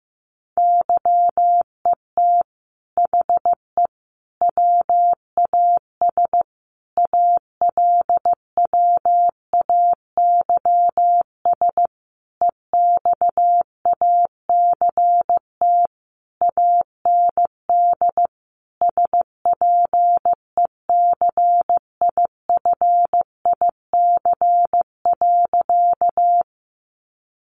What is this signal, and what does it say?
Morse code, heard as a keyed tone, about 15 wpm, 700 Hz. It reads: YET HE WAS ALWAYS EXACT AND SPECIFIC.
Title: Sentences from Top 1300 Words 15wpm